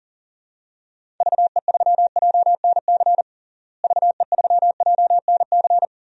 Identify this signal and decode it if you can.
Morse code, heard as a keyed tone, about 40 wpm, 700 Hz. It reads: VE3JNC VE3JNC
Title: DXCC-100 Call Signs - 6 Characters - Repeated 2x - 40wpm